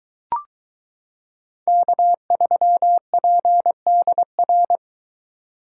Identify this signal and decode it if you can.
Morse code, heard as a keyed tone, about 22 wpm, 700 Hz. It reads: K3PDR